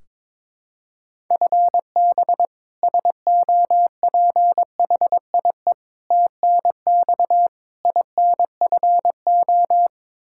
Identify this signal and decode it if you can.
Morse code, heard as a keyed tone, about 22 wpm, 700 Hz. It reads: FB SOPHIE TNX INFO
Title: Practice QSO - Exchange 3 - More Info - 22wpm